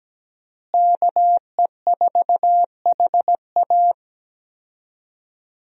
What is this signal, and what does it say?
Morse code, heard as a keyed tone, about 17 wpm, 700 Hz. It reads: KE4HA
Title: USA Call Signs - 5 Characters - Repeated 3x - 17wpm